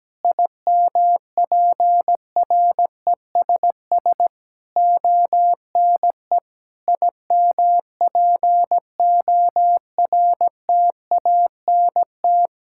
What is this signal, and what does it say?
Morse code, heard as a keyed tone, about 17 wpm, 700 Hz. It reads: IMPRESS ONE IMPORTANT
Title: Sentences from Top 1600 Words 17wpm